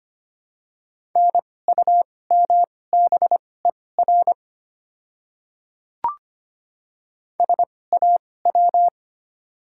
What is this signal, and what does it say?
Morse code, heard as a keyed tone, about 25 wpm, spaced as for 18 wpm, 700 Hz. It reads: NUMBER SAW
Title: 2x Character Spacing - Top 300 Words - 25wpm